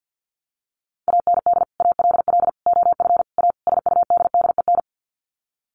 Morse code, heard as a keyed tone, about 50 wpm, 700 Hz. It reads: AND ALL OF A SUDDEN
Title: Sets of 5 Words 50wpm